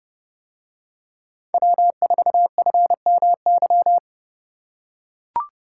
Morse code, heard as a keyed tone, about 30 wpm, 700 Hz. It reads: W4FMY